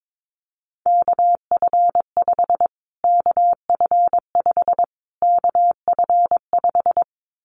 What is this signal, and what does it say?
Morse code, heard as a keyed tone, about 22 wpm, 700 Hz. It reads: KF5 KF5 KF5
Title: Callsign Prefixes for Top 25 DXCC - Repeated 3x - 22wpm